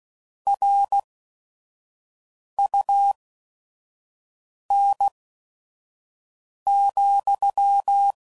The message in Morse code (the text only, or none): RUN,